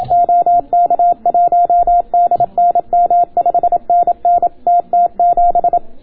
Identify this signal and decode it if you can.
Morse code, heard as a keyed tone, about 27 wpm, 690 Hz. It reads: OK1DNM 5NN TT7